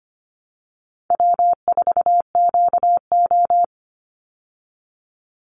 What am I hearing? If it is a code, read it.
Morse code, heard as a keyed tone, about 25 wpm, 700 Hz. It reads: W4QO
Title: Easy Call Signs Repeated 2x - 25wpm